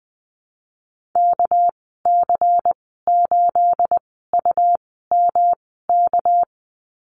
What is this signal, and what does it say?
Morse code, heard as a keyed tone, about 20 wpm, spaced as for 14 wpm, 700 Hz. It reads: KC8UMK
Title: 231 Lesson 37g - Letter x - Call Signs 20wpm